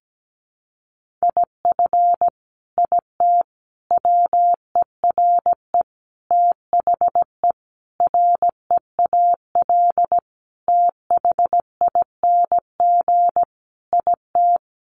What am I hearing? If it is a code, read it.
Morse code, heard as a keyed tone, about 17 wpm, 700 Hz. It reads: IF IT WERE THE REAL THING IT
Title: Sentences from Top 1300 Words 17wpm